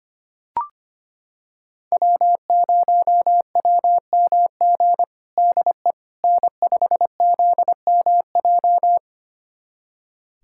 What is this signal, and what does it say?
Morse code, heard as a keyed tone, about 25 wpm, 700 Hz. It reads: W0WMG DE N5ZMJ
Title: USA Callsign DE Callsign - 5 Characters - 25wpm